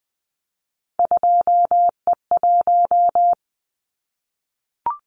Morse code, heard as a keyed tone, about 20 wpm, 700 Hz. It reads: 2E1
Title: Callsign Prefixes for Top 25 DXCC - 20wpm